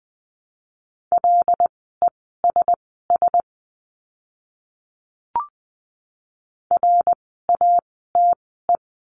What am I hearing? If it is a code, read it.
Morse code, heard as a keyed tone, about 20 wpm, spaced as for 14 wpm, 700 Hz. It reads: LESS RATE
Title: 094 Lesson 16f - Letter c - Words 20wpm